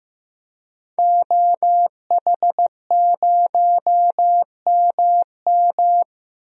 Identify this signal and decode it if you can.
Morse code, heard as a keyed tone, about 15 wpm, 700 Hz. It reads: OH0MM
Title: DXCC-100 Call Signs - 5 Characters - 15wpm